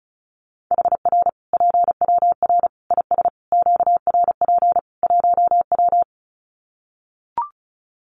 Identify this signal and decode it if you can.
Morse code, heard as a keyed tone, about 34 wpm, 700 Hz. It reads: HR PWR IS QRP 1W
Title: Common QSO Phrases 35wpm